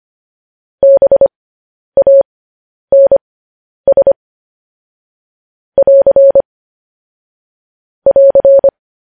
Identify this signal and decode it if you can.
Morse code, heard as a keyed tone, about 25 wpm, spaced as for 10 wpm, 550 Hz. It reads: BANS <AR> <AR>